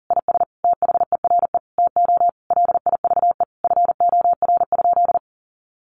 Morse code, heard as a keyed tone, about 40 wpm, 700 Hz. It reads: IS THERE TO LIVE FOR?
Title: Sentences from Top 200 Words 40wpm